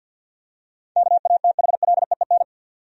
Morse code, heard as a keyed tone, about 51 wpm, 700 Hz. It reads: KATHLEEN
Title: USA Common Names - 50wpm